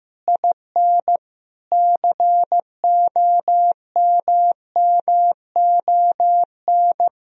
Morse code, heard as a keyed tone, about 15 wpm, 700 Hz. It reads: IN COMMON
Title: Sentences from Top 1000 Words - Review Part 3 - 15wpm